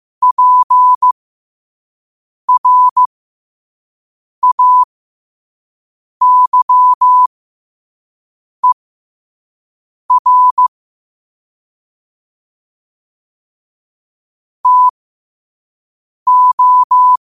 Morse code, heard as a keyed tone, about 15 wpm, spaced as for 5 wpm, 1000 Hz. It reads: PRAYER TO
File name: qotdfiles / 5wpm